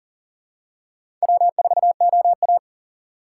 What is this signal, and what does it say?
Morse code, heard as a keyed tone, about 42 wpm, 700 Hz. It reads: W4OA
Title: USA Call Signs - 4 Characters - 4x Character Spacing - 40wpm